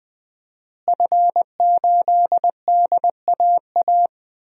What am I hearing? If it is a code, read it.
Morse code, heard as a keyed tone, about 20 wpm, 700 Hz. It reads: F8DAA
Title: DXCC-100 Call Signs - 5 Characters - Repeated 3x - 20wpm